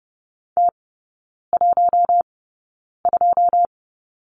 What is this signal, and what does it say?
Morse code, heard as a keyed tone, about 30 wpm, spaced as for 9 wpm, 700 Hz. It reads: T12
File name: 081 Lesson 14e - Number 2 - Three Characters 30wpm